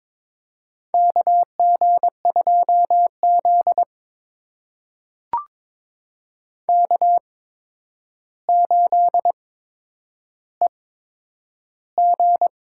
Morse code, heard as a keyed tone, about 21 wpm, spaced as for 9 wpm, 700 Hz. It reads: KG2Z K8EG
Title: USA Call Signs - 4 Characters - 8x Character Spacing - 22wpm